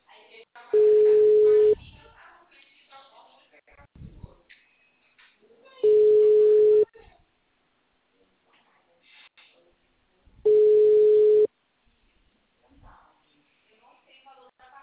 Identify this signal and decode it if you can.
Ringback tone (Europe, 425 Hz), with speech.